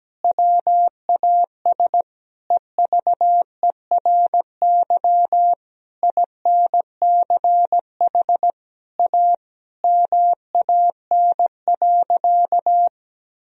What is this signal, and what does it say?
Morse code, heard as a keyed tone, about 17 wpm, 700 Hz. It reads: WAS EVERY INCH A MAN.
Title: Sentences from Top 1000 Words - Review Part 3 - 17wpm